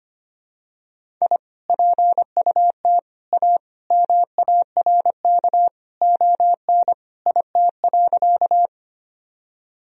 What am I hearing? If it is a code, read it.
Morse code, heard as a keyed tone, about 25 wpm, 700 Hz. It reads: I PUT A MARK ON IT.
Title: Sentences from Top 1000 Words - Review Part 1 - 25wpm